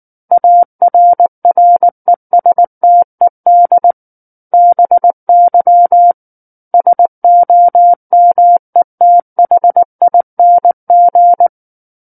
Morse code, heard as a keyed tone, about 19 wpm, 700 Hz. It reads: ARRESTED BY SOMETHING